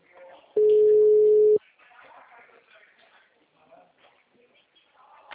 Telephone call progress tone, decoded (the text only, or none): ringback tone